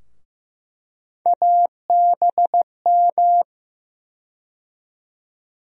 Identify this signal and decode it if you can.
Morse code, heard as a keyed tone, about 16 wpm, 700 Hz. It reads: ABM
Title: USA Call Sign Postfixes - 3 Letters 15wpm